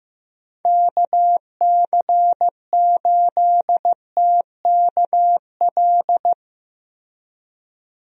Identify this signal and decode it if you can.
Morse code, heard as a keyed tone, about 16 wpm, 700 Hz. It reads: KC8TKL
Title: USA Call Signs - 6 Characters - Repeated 3x - 15wpm